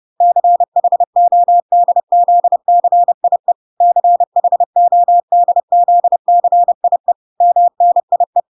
Morse code, heard as a keyed tone, about 30 wpm, 700 Hz. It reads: CHODZCIE CHODZCIE MNIE